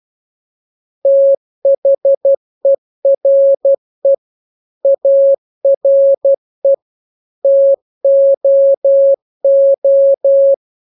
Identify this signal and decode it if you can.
Morse code, heard as a keyed tone, about 12 wpm, 550 Hz. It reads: THERE ARE TOO